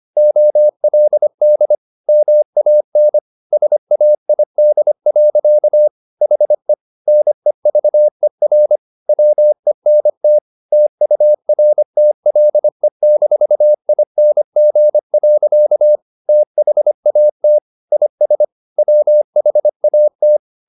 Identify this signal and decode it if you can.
Morse code, heard as a keyed tone, about 25 wpm, 600 Hz. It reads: OLD MAN SAID. HE NEVER WENT TURTLE-ING. THAT IS WHAT
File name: the old man and the sea part 1 25wpm